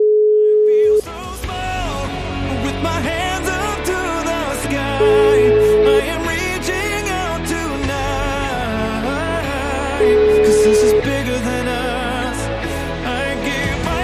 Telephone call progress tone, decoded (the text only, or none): ringback tone